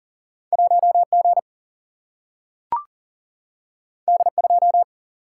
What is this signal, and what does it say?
Morse code, heard as a keyed tone, about 40 wpm, 700 Hz. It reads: 1G D2
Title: Single-Letter Single-Number 40wpm